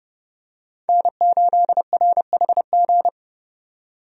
Morse code, heard as a keyed tone, about 30 wpm, 700 Hz. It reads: N8RHG
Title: USA Call Signs - 5 Characters - 30wpm